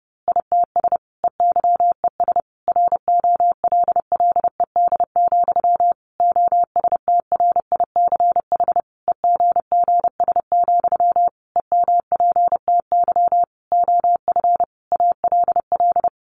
Morse code, heard as a keyed tone, about 30 wpm, 700 Hz. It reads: ITS EYES ROLLED, OSTRICH EGGS, EMPTY OF ALL
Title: A Sound Of Thunder 30wpm